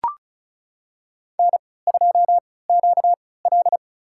Morse code, heard as a keyed tone, about 36 wpm, spaced as for 20 wpm, 700 Hz. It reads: N2QL
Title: USA Call Signs - 4 Characters - 3x Character Spacing - 35wpm